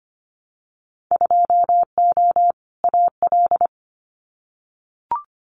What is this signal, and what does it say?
Morse code, heard as a keyed tone, about 25 wpm, 700 Hz. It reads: 2O AL
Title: Winter Field Day Exchanges Encore 25wpm